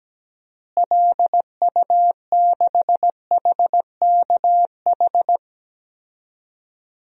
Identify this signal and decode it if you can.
Morse code, heard as a keyed tone, about 17 wpm, 700 Hz. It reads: LU6HKH
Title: DXCC-100 Call Signs - 6 Characters - Repeated 3x - 17wpm